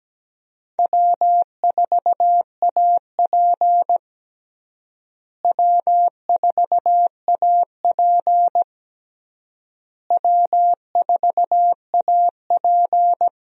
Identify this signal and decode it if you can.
Morse code, heard as a keyed tone, about 17 wpm, 700 Hz. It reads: W4AP W4AP W4AP
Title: USA Call Signs - 4 Characters - Repeated 3x - 17wpm